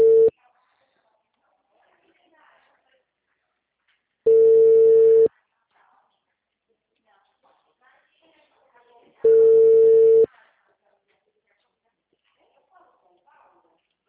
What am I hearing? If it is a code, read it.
Ringback tone (China, 450 Hz), with speech.